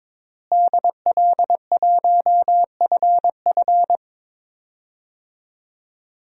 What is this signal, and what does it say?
Morse code, heard as a keyed tone, about 22 wpm, 700 Hz. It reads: DL1FF
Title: DXCC-100 Call Signs - 5 Characters - Repeated 2x - 22wpm